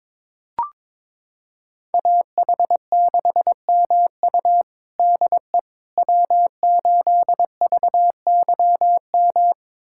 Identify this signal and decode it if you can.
Morse code, heard as a keyed tone, about 22 wpm, 700 Hz. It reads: AH6MU DE W8VYM
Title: USA Callsign DE Callsign - 5 Characters - 22wpm